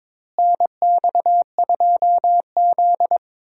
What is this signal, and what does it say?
Morse code, heard as a keyed tone, about 21 wpm, 700 Hz. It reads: NX2Z
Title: USA Call Signs - 4 Characters - Repeated 3x - 22wpm